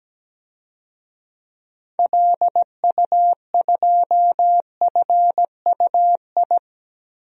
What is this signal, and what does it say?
Morse code, heard as a keyed tone, about 17 wpm, 700 Hz. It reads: LU2FUI
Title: DXCC-100 Call Signs - 6 Characters - 17wpm